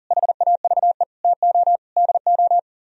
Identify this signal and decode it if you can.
Morse code, heard as a keyed tone, about 53 wpm, 700 Hz. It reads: HAVE TO DO